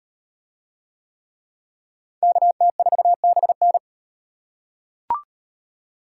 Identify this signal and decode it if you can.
Morse code, heard as a keyed tone, about 38 wpm, 700 Hz. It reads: KT4BN